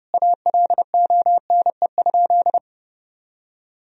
Morse code, heard as a keyed tone, about 32 wpm, 700 Hz. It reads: ALONE?
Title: Sentences from Top 1000 Words - Review Part 2 - 30wpm